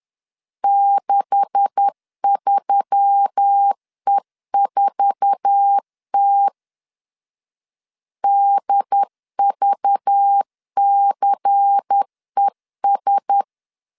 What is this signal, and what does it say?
Morse code, heard as a keyed tone, about 10 wpm, 800 Hz. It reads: 63E4T DVCES
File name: CW / DET / 1 9 1500C S B 10wpm 4439